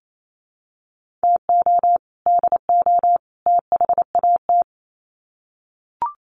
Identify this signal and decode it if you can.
Morse code, heard as a keyed tone, about 28 wpm, 700 Hz. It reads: TO DO THAT
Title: Sets of 3 Words 28wpm